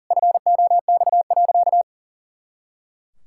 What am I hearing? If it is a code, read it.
Morse code, heard as a keyed tone, about 37 wpm, 700 Hz. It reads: FOX.
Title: Pangram Challenge v2 - 40wpm